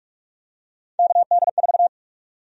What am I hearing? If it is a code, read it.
Morse code, heard as a keyed tone, about 46 wpm, 700 Hz. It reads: KD4
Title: USA Call Sign Prefixes - Repeated 2x - 45wpm